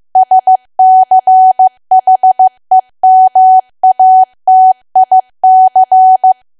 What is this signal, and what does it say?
Morse code, heard as a keyed tone, about 15 wpm, 750 Hz. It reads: SCHEMATIC
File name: cwa / W1AW-Level2 / 15WPM